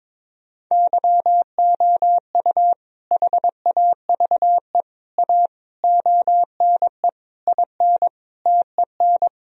Morse code, heard as a keyed tone, about 22 wpm, 700 Hz. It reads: YOU HAVE A ONE IN TEN
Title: Sentences from Top 1000 Words - Review Part 2 - 22wpm